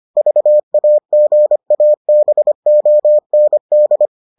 Morse code, heard as a keyed tone, about 25 wpm, 600 Hz. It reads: VAGABOND